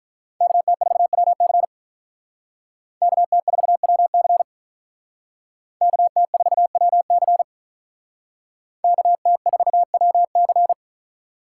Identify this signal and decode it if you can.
Morse code, heard as a keyed tone, about 45 wpm, 700 Hz. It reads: KT4WC KT4WC KT4WC KT4WC